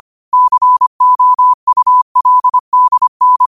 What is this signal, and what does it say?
Morse code, heard as a keyed tone, about 25 wpm, 1000 Hz. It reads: COULDN
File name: qotdfiles / 25wpm